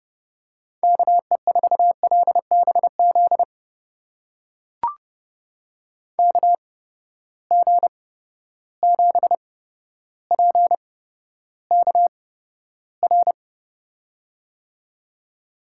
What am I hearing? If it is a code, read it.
Morse code, heard as a keyed tone, about 30 wpm, spaced as for 13 wpm, 700 Hz. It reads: KE4LBZ KG7PKR
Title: USA Call Signs - 6 Characters - 8x Character Spacing - 30wpm